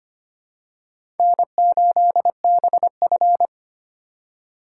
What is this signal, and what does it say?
Morse code, heard as a keyed tone, about 25 wpm, 700 Hz. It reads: N8BF
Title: USA Call Signs - 4 Characters - 3x Character Spacing - 25wpm